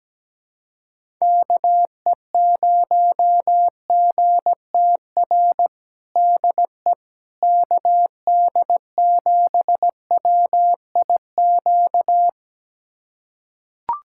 Morse code, heard as a keyed tone, about 17 wpm, 700 Hz. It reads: KE0GTR DE KD7WIQ
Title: USA Callsign DE Callsign - 6 Characters - 17wpm